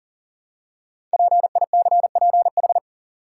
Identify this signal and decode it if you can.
Morse code, heard as a keyed tone, about 40 wpm, 700 Hz. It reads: PICPH
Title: Code Groups - 5 Letters - 40wpm